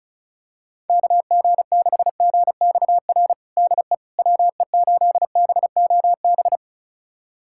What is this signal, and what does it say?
Morse code, heard as a keyed tone, about 36 wpm, 700 Hz. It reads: KG6GXR DE WE8BOB
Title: USA Callsign DE Callsign 35wpm